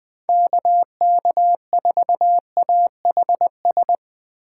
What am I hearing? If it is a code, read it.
Morse code, heard as a keyed tone, about 20 wpm, 700 Hz. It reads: KK4AHS